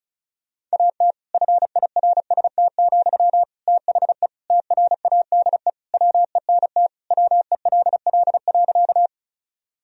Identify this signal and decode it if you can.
Morse code, heard as a keyed tone, about 36 wpm, 700 Hz. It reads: AT FIRST, THE TRADE WENT WELL.
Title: Sentences from Top 1000 Words - Review Part 4 - 35wpm